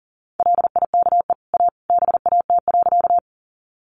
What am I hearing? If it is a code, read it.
Morse code, heard as a keyed tone, about 40 wpm, 700 Hz. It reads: LIKE A BAT.
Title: Sentences from Top 1500 Words 40wpm